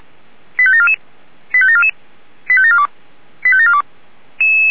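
Five-tone selective call (ZVEI), heard as quick runs of five tones, tones of about 70 ms. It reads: 85744 85744 85742 85742
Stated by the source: ZVEI five-tone selective call received by SDR, feuerwehr-orlamuende.org fire brigade.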